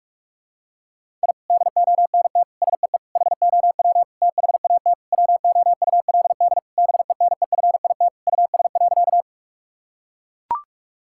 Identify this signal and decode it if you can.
Morse code, heard as a keyed tone, about 44 wpm, 700 Hz. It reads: I DONT SEE HOW THAT WOULD BENEFIT US.
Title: Sentences from Top 1000 Words - Review Part 6 - 45wpm